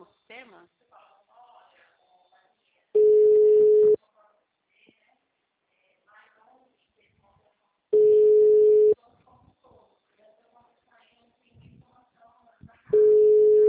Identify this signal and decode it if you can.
Ringback tone (Europe, 425 Hz), with speech.
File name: out- -2016- - - .17109